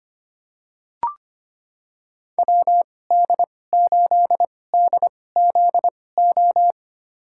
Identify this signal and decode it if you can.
Morse code, heard as a keyed tone, about 25 wpm, spaced as for 18 wpm, 700 Hz. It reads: WD8DZO
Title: USA Call Signs - 2x Character Spacing - 25wpm